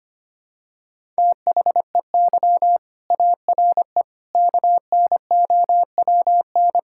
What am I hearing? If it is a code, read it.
Morse code, heard as a keyed tone, about 25 wpm, 700 Hz. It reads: THEY ARE KNOWN
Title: Sentences from Top 1000 Words - Review Part 5 - 25wpm